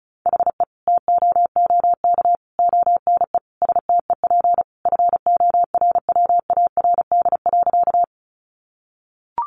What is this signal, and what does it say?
Morse code, heard as a keyed tone, about 34 wpm, 700 Hz. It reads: HE TOOK ONE STEP FORWARD.
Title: Sentences from Top 1300 Words 35wpm